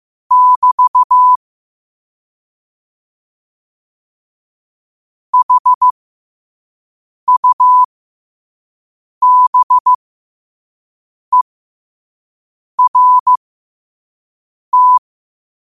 Morse code, heard as a keyed tone, about 15 wpm, spaced as for 5 wpm, 1000 Hz. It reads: <BT> HUBERT
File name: qotdfiles / 5wpm